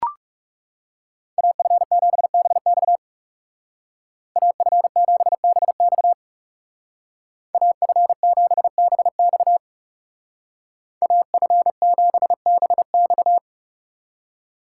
Morse code, heard as a keyed tone, about 35 wpm, 700 Hz. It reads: AF7BX AF7BX AF7BX AF7BX
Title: Speed Racing - USA 2x2 Call Signs 30wpm